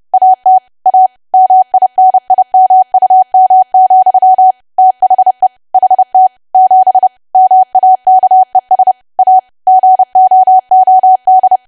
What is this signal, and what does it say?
Morse code, heard as a keyed tone, about 30 wpm, 750 Hz. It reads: AT A MINIMUM, THE HT 7 MAKES A GOOD
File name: cwa / W1AW-Level3 / 30WPM